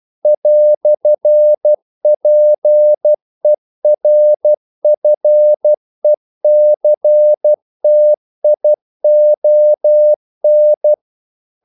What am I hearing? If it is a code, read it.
Morse code, heard as a keyed tone, about 12 wpm, 600 Hz. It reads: "PERFECTION
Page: English quotations in CW (telegraphy - morse)